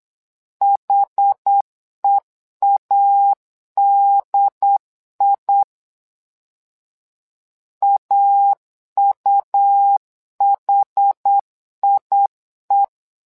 Morse code, heard as a keyed tone, about 8 wpm, 800 Hz. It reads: HEADI AUHIE
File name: LEZIONI / 2 UHDN / 2 UHD 8wpm 2258